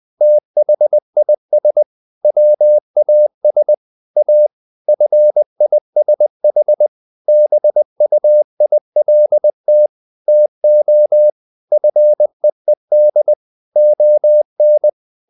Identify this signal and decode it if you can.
Morse code, heard as a keyed tone, about 20 wpm, 600 Hz. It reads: THIS WAS A FISH BUILT TO FEED ON
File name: the old man and the sea part 2 20wpm